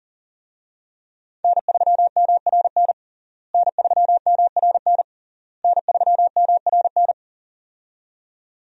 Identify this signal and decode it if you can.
Morse code, heard as a keyed tone, about 40 wpm, 700 Hz. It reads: N3MRN N3MRN N3MRN
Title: USA Call Signs - 5 Characters - Repeated 3x - 40wpm